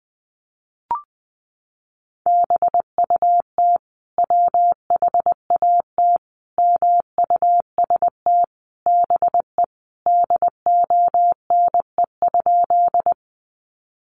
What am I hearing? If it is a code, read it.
Morse code, heard as a keyed tone, about 20 wpm, 700 Hz. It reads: BUT WHAT MUST BE DONE?
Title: Sentences from Top 500 Words 20wpm